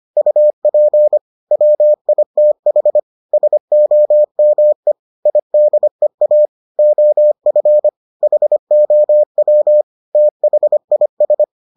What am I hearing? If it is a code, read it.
Morse code, heard as a keyed tone, about 25 wpm, 600 Hz. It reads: UP WITH SOME IDEA OF HOW THIS